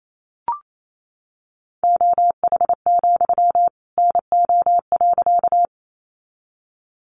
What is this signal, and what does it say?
Morse code, heard as a keyed tone, about 28 wpm, 700 Hz. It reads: OH, NO.
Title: Sentences from Top 500 Words 28wpm